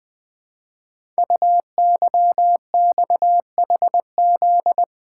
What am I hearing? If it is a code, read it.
Morse code, heard as a keyed tone, about 20 wpm, 700 Hz. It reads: UYXHZ
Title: Code Groups - 5 Letters - 20wpm